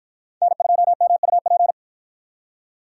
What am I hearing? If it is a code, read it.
Morse code, heard as a keyed tone, about 53 wpm, 700 Hz. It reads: N2MUP